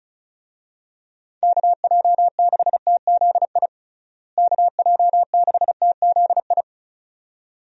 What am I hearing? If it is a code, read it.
Morse code, heard as a keyed tone, about 36 wpm, 700 Hz. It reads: KJ6TZI KJ6TZI
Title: USA Call Signs - 6 Characters - Repeated 2x - 35wpm